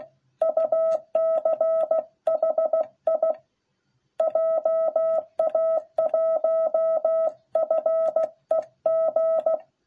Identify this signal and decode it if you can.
Morse code, heard as a keyed tone, about 16 wpm, 660 Hz. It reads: UCHI JA1FEG